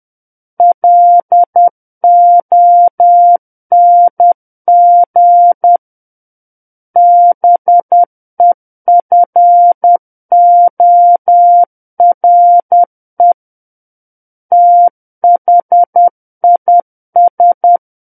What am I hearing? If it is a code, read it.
Morse code, heard as a keyed tone, about 10 wpm, 700 Hz. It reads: LONG BEFORE THIS
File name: Book One / 01-Chapter 1 - 10 WPM